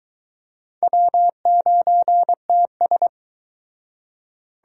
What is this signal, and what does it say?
Morse code, heard as a keyed tone, about 24 wpm, 700 Hz. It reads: W9TS